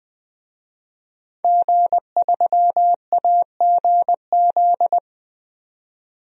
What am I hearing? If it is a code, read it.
Morse code, heard as a keyed tone, about 20 wpm, 700 Hz. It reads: G3AGZ
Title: DXCC-100 Call Signs - 5 Characters - 20wpm